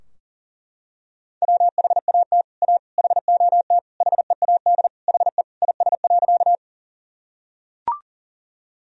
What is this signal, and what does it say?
Morse code, heard as a keyed tone, about 40 wpm, 700 Hz. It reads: WHAT A HOT HEAD HE IS.
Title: Sentences from Top 1000 Words - Review Part 3 - 40wpm